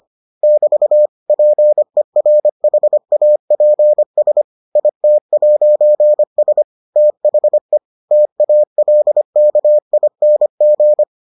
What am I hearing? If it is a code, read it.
Morse code, heard as a keyed tone, about 25 wpm, 600 Hz. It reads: <BT> PERHAPS IT'S THE TALKING